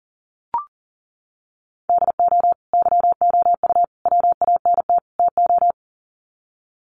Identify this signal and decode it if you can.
Morse code, heard as a keyed tone, about 40 wpm, 700 Hz. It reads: DO YOU WANT TO
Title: Sets of 4 Words 40wpm